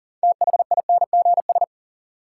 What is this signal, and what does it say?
Morse code, heard as a keyed tone, about 40 wpm, 700 Hz. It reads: THINGS
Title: Sentences from Top 1000 Words - Review Part 4 - New Word Drill - 40wpm